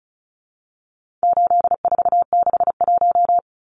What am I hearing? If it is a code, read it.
Morse code, heard as a keyed tone, about 34 wpm, 700 Hz. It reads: 8461
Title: Numbers - 4-Digit 35wpm